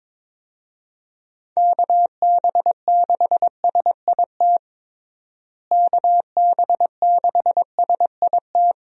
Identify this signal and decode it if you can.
Morse code, heard as a keyed tone, about 22 wpm, 700 Hz. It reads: KB6SIT KB6SIT
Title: USA Call Signs - 6 Characters - Repeated 3x - 22wpm